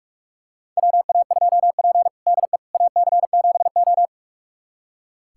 Morse code, heard as a keyed tone, about 44 wpm, 690 Hz. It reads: WA2P DE AC7O
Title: USA Callsign DE Callsign - 4 Characters - 45wpm